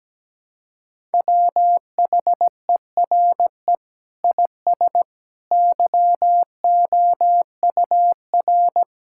Morse code, heard as a keyed tone, about 17 wpm, 700 Hz. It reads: WHERE IS YOUR
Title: Sentences from Top 1000 Words - Review Part 5 - 17wpm